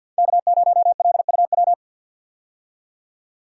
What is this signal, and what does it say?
Morse code, heard as a keyed tone, about 48 wpm, 700 Hz. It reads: K0LUW
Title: K1USN SST QSO Practice - 50wpm